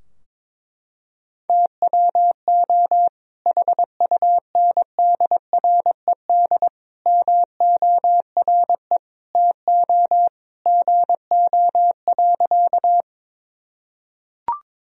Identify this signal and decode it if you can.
Morse code, heard as a keyed tone, about 21 wpm, 700 Hz. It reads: TWO HUNDRED MORE TO GO.
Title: Sentences from Top 1000 Words - Review Part 2 - 22wpm